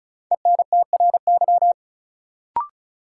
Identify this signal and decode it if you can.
Morse code, heard as a keyed tone, about 36 wpm, 700 Hz. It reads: ENTRY
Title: Sentences from Top 1500 Words - New Word Drill Alt - 35wpm